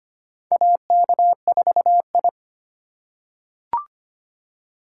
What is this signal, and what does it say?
Morse code, heard as a keyed tone, about 25 wpm, 700 Hz. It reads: AK4I
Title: USA Call Signs - 3x Character Spacing - 25wpm